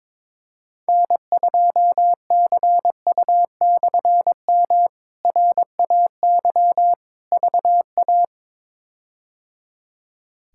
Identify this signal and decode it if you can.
Morse code, heard as a keyed tone, about 22 wpm, 700 Hz. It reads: N2CU/M RAY VA
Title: K1USN SST QSO Practice - 22wpm